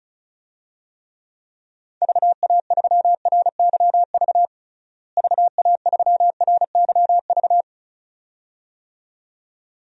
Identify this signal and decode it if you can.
Morse code, heard as a keyed tone, about 36 wpm, 700 Hz. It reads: VA3RYV VA3RYV
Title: DXCC-100 Call Signs - 6 Characters - Repeated 2x - 35wpm